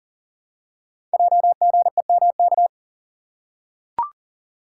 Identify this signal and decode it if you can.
Morse code, heard as a keyed tone, about 40 wpm, 690 Hz. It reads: JGEMK
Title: Code Groups - 5 Letters - 40wpm